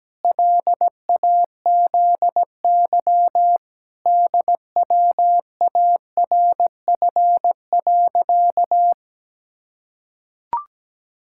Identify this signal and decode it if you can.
Morse code, heard as a keyed tone, about 17 wpm, 700 Hz. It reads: LAZY DWARF.
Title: Pangram Challenge v2 - 17wpm